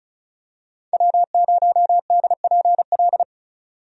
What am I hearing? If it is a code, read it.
Morse code, heard as a keyed tone, about 34 wpm, 700 Hz. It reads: W0DPL
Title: USA Call Signs - 5 Characters - 35wpm